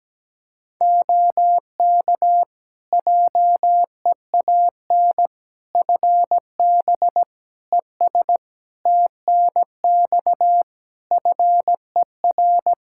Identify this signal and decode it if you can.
Morse code, heard as a keyed tone, about 17 wpm, 700 Hz. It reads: OK JEAN FB ES TNX FER
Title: Practice QSO - Exchange 2 - Info - 17wpm